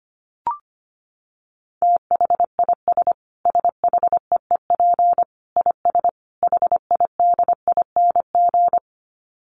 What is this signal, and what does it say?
Morse code, heard as a keyed tone, about 25 wpm, 700 Hz. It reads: THIS SHEEP IS HIDING
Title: Dit Challenge 25wpm